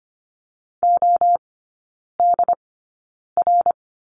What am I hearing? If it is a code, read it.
Morse code, heard as a keyed tone, about 25 wpm, spaced as for 9 wpm, 690 Hz. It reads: ODR
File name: Lesson 15e - Number 5 - Three Characters 25wpm